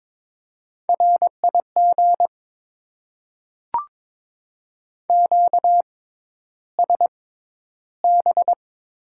Morse code, heard as a keyed tone, about 22 wpm, spaced as for 11 wpm, 700 Hz. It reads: RIG QSB
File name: Top 90 QSO Elements - 6x Character Spacing - 22wpm